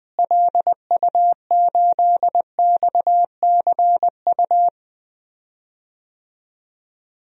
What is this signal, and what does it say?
Morse code, heard as a keyed tone, about 20 wpm, 700 Hz. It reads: LU8XCU
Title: DXCC-100 Call Signs - 6 Characters - Repeated 2x - 20wpm